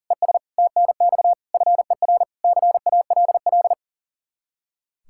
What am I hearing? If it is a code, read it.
Morse code, heard as a keyed tone, about 40 wpm, 700 Hz. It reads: ES TNX FER CALL
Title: K1USN SST QSO Practice 40wpm